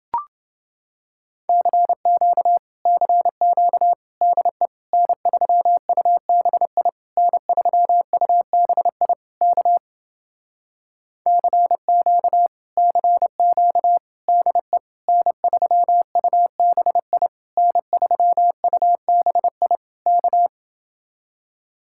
Morse code, heard as a keyed tone, about 29 wpm, 700 Hz. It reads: CQ CQ DE N3UBI N3UBI K CQ CQ DE N3UBI N3UBI K